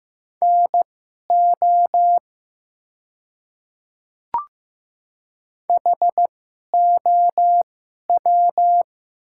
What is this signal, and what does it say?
Morse code, heard as a keyed tone, about 15 wpm, spaced as for 11 wpm, 690 Hz. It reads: NO HOW
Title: 2x Character Spacing - Top 100 Words - 15wpm